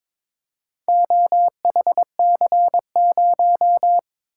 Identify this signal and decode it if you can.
Morse code, heard as a keyed tone, about 21 wpm, 700 Hz. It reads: OHC0
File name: practice / 22wpm / Four Letter-Number Combinations - 22wpm